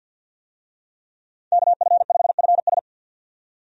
Morse code, heard as a keyed tone, about 48 wpm, 700 Hz. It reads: KF5FS
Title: USA Call Signs - 5 Characters - 50wpm